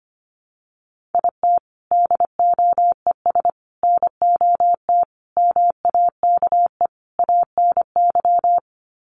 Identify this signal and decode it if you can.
Morse code, heard as a keyed tone, about 25 wpm, 700 Hz. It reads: IT DOES NOT MAKE ANY
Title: Sets of 5 Words Encore2 25wpm